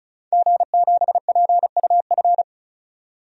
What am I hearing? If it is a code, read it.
Morse code, heard as a keyed tone, about 34 wpm, 700 Hz. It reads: G7PUF
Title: DXCC-100 Call Signs - 5 Characters - 35wpm